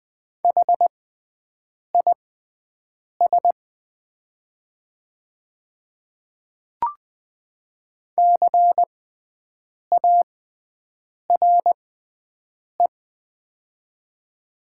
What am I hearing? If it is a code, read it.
Morse code, heard as a keyed tone, about 20 wpm, spaced as for 7 wpm, 690 Hz. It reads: HIS CARE
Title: 6x Character Spacing - Top 300 Words - 20wpm